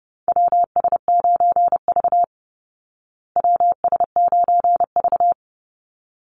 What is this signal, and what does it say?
Morse code, heard as a keyed tone, about 30 wpm, 700 Hz. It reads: WS9V WS9V
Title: Call Signs Repeated 3x - 30wpm